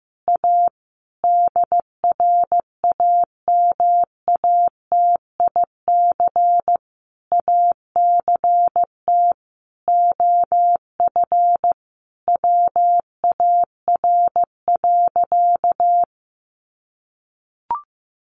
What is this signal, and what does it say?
Morse code, heard as a keyed tone, about 15 wpm, 700 Hz. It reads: A DRAMATIC ACT OF WAR.
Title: Sentences from Top 1300 Words 15wpm